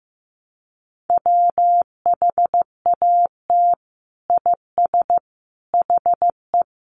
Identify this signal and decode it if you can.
Morse code, heard as a keyed tone, about 15 wpm, 700 Hz. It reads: WHAT IS HE
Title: Sentences from Top 100 Words 15wpm